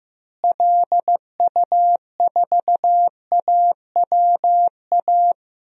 Morse code, heard as a keyed tone, about 15 wpm, 700 Hz. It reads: LU4AWA
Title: DXCC-100 Call Signs - 6 Characters - Repeated 3x - 15wpm